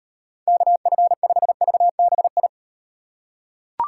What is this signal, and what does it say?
Morse code, heard as a keyed tone, about 38 wpm, 700 Hz. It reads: KF5VBI